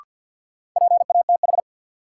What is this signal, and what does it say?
Morse code, heard as a keyed tone, about 49 wpm, 700 Hz. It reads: PATH
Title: Sentences from Top 1400 Words - New Word Drill - 50wpm